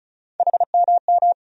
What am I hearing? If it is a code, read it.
Morse code, heard as a keyed tone, about 36 wpm, 700 Hz. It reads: HMM
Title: USA Call Sign Postfixes - 3 Letters 35wpm